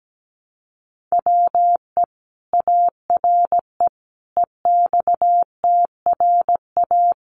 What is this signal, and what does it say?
Morse code, heard as a keyed tone, about 17 wpm, 700 Hz. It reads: WE ARE EXTRA
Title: Sentences from Top 1000 Words 17wpm